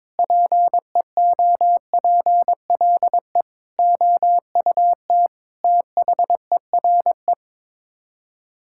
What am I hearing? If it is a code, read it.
Morse code, heard as a keyed tone, about 21 wpm, 700 Hz. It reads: PEOPLE OUT THERE
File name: practice / 22wpm / Sets of 8 Words - 22wpm